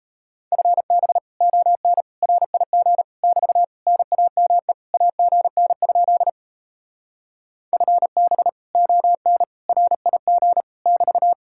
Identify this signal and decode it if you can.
Morse code, heard as a keyed tone, about 38 wpm, 700 Hz. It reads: FB ON RIG <BT> NAME AGN? FB ON RIG <BT>